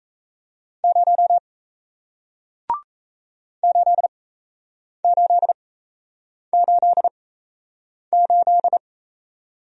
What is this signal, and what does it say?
Morse code, heard as a keyed tone, about 36 wpm, spaced as for 8 wpm, 700 Hz. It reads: 0 8888